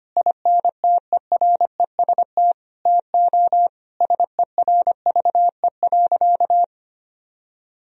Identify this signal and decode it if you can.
Morse code, heard as a keyed tone, about 25 wpm, 700 Hz. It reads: INTEREST TO SERVE.
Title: Sentences from Top 1000 Words - Review Part 4 - 25wpm